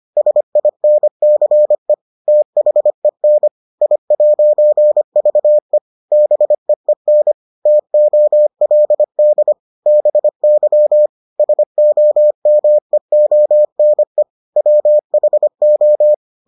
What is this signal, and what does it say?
Morse code, heard as a keyed tone, about 25 wpm, 600 Hz. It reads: SINCE THEN I'VE BEEN TOLD BY SOMEONE WHO